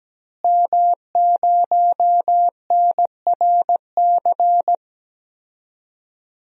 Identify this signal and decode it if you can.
Morse code, heard as a keyed tone, about 17 wpm, 700 Hz. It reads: M0NRC
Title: DXCC-100 Call Signs - 5 Characters - Repeated 3x - 17wpm